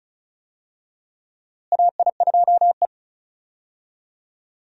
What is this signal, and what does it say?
Morse code, heard as a keyed tone, about 34 wpm, 700 Hz. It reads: AI2E
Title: USA Call Signs - 4 Characters - Repeated 3x - 35wpm